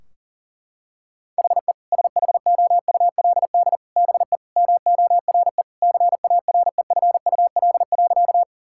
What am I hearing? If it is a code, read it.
Morse code, heard as a keyed tone, about 40 wpm, 700 Hz. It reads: HE SHOULD BE MORE CAREFUL.
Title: Sentences from Top 1600 Words 40wpm